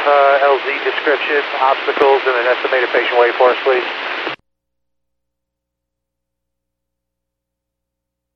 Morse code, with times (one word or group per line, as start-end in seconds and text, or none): none